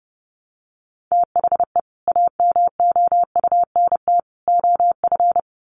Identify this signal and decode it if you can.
Morse code, heard as a keyed tone, about 30 wpm, 700 Hz. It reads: THE AMOUNT OF
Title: Sets of 3 Words 30wpm